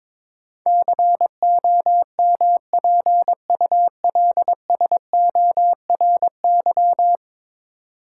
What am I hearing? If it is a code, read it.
Morse code, heard as a keyed tone, about 22 wpm, 700 Hz. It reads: COMPULSORY
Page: morse-code-ninja-practice-22wpm